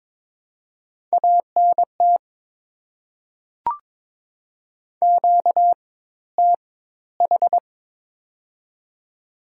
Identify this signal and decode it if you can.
Morse code, heard as a keyed tone, about 22 wpm, spaced as for 14 wpm, 690 Hz. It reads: ANT QTH
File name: Top 20 QSO Elements - 4x Character Spacing - 22wpm